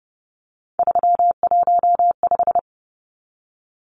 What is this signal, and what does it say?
Morse code, heard as a keyed tone, about 30 wpm, 700 Hz. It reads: 315
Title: Numbers - 3-Digit 30wpm